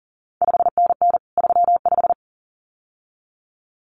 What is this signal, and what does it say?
Morse code, heard as a keyed tone, about 41 wpm, 700 Hz. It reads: 5NN 35
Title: CQ WW DX Contest Exchange Encore - 40wpm